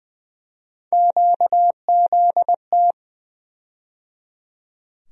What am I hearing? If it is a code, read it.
Morse code, heard as a keyed tone, about 20 wpm, 700 Hz. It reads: QZT
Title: USA Call Sign Postfixes - 3 Letters 20wpm